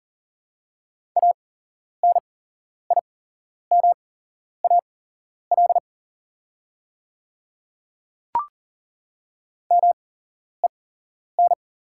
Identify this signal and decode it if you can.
Morse code, heard as a keyed tone, about 37 wpm, spaced as for 11 wpm, 700 Hz. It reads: ANIMAL MEN
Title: 8x Character Spacing - Top 200 Words - 40wpm